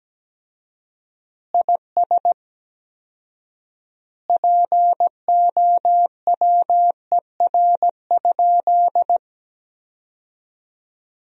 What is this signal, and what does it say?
Morse code, heard as a keyed tone, about 17 wpm, 700 Hz. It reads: IS POWER?
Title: 4x Word-Spacing - Sentences from Top 500 Words - 17wpm